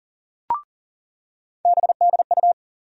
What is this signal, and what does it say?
Morse code, heard as a keyed tone, about 40 wpm, 700 Hz. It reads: BDU
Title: USA Call Sign Postfixes - 3 Letters 40wpm